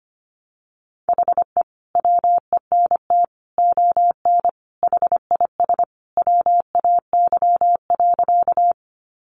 Morse code, heard as a keyed tone, about 25 wpm, 700 Hz. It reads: HE WENT ON HIS WAY.
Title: Sentences from Top 200 Words 25wpm